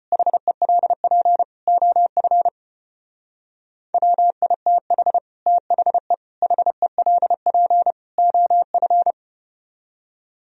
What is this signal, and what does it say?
Morse code, heard as a keyed tone, about 33 wpm, 700 Hz. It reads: HELP OF WITH THE HELP OF